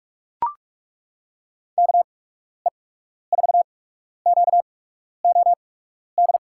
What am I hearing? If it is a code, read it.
Morse code, heard as a keyed tone, about 42 wpm, spaced as for 12 wpm, 700 Hz. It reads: KE4QOD
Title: USA Call Signs - 6 Characters - 8x Character Spacing - 45wpm